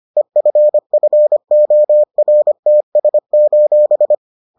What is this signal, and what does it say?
Morse code, heard as a keyed tone, about 25 wpm, 600 Hz. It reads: EFFORTS: